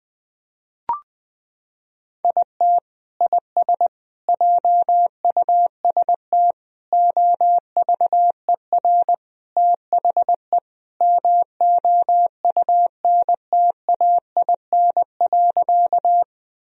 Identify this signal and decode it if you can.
Morse code, heard as a keyed tone, about 20 wpm, 700 Hz. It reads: IT IS JUST OVER THE MOUNTAIN.
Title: Sentences from Top 1000 Words - Review Part 1 - 20wpm